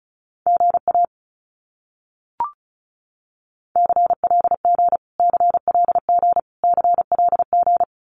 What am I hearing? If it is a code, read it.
Morse code, heard as a keyed tone, about 34 wpm, 700 Hz. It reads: GA CLG CLG CLG
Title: Abbreviations Repeated 3x - 35wpm